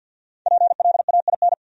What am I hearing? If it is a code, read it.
Morse code, heard as a keyed tone, about 67 wpm, spaced as for 58 wpm, 700 Hz. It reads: PLAIN
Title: Speed-Racing Top 500 Words 50wpm